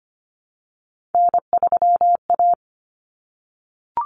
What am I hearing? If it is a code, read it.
Morse code, heard as a keyed tone, about 25 wpm, 700 Hz. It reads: N3A
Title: Common Cut Numbers - 3-Digit 25wpm